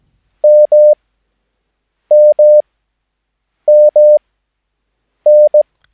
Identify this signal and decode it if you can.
Morse code, heard as a keyed tone, about 17 wpm, spaced as for 7 wpm, 600 Hz. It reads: MMMN
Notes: Övningsmaterial med alfabetet A till Ö samt siffror 0 till 9 enligt International/Continental Morse Code.